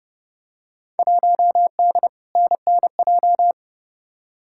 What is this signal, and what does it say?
Morse code, heard as a keyed tone, about 30 wpm, 700 Hz. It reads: 1D NNJ
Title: Field Day Exchanges - 30wpm